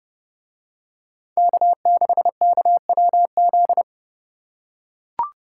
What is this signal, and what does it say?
Morse code, heard as a keyed tone, about 29 wpm, 700 Hz. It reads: K6KWZ